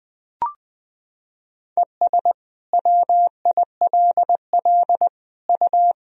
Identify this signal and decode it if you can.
Morse code, heard as a keyed tone, about 20 wpm, 700 Hz. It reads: ES WILL U
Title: Sentences from Top 400 Words and Abbreviations 20wpm